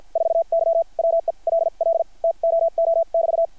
Morse code, heard as a keyed tone, about 50 wpm, spaced as for 41 wpm, 660 Hz. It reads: 4Y WELL TOO<BT>
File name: archived / CRSnet